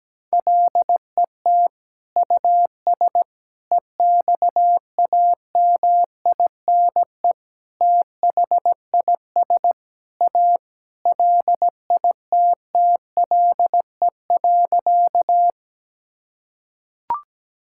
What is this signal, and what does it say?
Morse code, heard as a keyed tone, about 17 wpm, 700 Hz. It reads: LET US EXAMINE THIS A LITTLE.
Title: Sentences from Top 1600 Words 17wpm